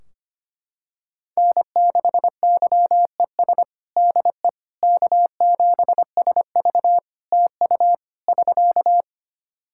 Morse code, heard as a keyed tone, about 25 wpm, 700 Hz. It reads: N6YES DE K7SV TU <SK>
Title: K1USN SST QSO Practice 25wpm